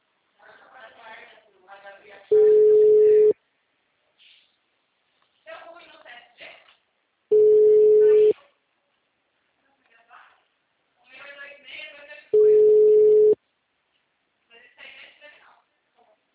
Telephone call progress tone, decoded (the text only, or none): ringback tone